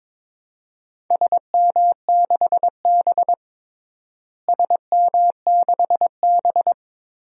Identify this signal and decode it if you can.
Morse code, heard as a keyed tone, about 22 wpm, 700 Hz. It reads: SM6B SM6B
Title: DXCC-100 Call Signs - 4 Characters - Repeated 2x - 22wpm